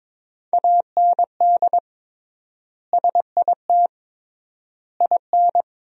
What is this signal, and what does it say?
Morse code, heard as a keyed tone, about 21 wpm, 700 Hz. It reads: AND SIT IN
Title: 3x Word-Spacing - Sentences from Top 500 Words - 22wpm